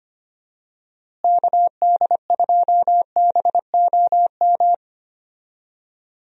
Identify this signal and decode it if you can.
Morse code, heard as a keyed tone, about 25 wpm, 700 Hz. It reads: KD2BOM
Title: USA Call Signs - 6 Characters - Repeated 3x - 25wpm